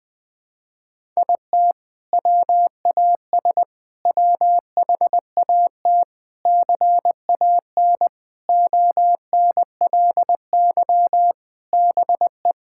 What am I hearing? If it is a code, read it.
Morse code, heard as a keyed tone, about 20 wpm, 700 Hz. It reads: IT WAS WHAT CAN ONLY BE
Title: Sentences with 6-Word Phrases - 20wpm